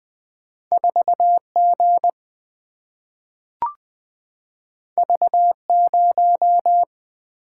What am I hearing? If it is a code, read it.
Morse code, heard as a keyed tone, about 20 wpm, 700 Hz. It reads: 4G V0
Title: Single-Letter Single-Number 20wpm